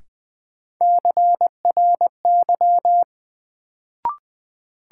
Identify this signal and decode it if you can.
Morse code, heard as a keyed tone, about 20 wpm, 700 Hz. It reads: CRY
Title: Sentences from Top 500 Words - New Word Drill - 20wpm